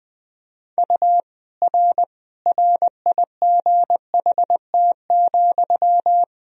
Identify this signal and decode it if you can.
Morse code, heard as a keyed tone, about 21 wpm, 700 Hz. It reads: U R RIGHT,
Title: Sentences from Top 400 Words and Abbreviations 20wpm